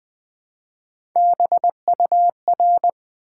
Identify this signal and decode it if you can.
Morse code, heard as a keyed tone, about 20 wpm, 700 Hz. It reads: BUR
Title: Build A Word - 1 Letter At A Time - 5-Letter Words - 20wpm